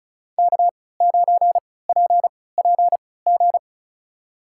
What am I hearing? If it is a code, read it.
Morse code, heard as a keyed tone, about 36 wpm, spaced as for 20 wpm, 700 Hz. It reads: K9PPG
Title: USA Call Signs - 5 Characters - 3x Character Spacing - 35wpm